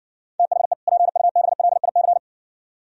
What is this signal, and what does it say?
Morse code, heard as a keyed tone, about 60 wpm, 690 Hz. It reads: THE PUBLIC